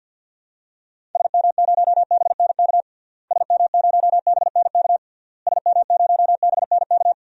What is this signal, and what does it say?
Morse code, heard as a keyed tone, about 51 wpm, 700 Hz. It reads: SM0BNK SM0BNK SM0BNK
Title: DXCC-100 Call Signs - 6 Characters - Repeated 3x - 50wpm